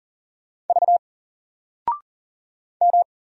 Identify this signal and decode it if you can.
Morse code, heard as a keyed tone, about 39 wpm, 700 Hz. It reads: V M
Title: Single Letter-Number 40wpm